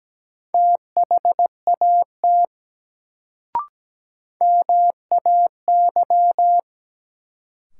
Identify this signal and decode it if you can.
Morse code, heard as a keyed tone, about 17 wpm, 700 Hz. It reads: THAT MAY
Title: Sets of 2 Words - New Word Drill - 17wpm